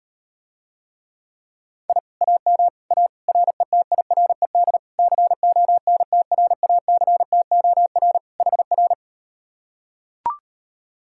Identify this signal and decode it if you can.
Morse code, heard as a keyed tone, about 38 wpm, 700 Hz. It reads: I AM A RETIRED CONTRACTOR HR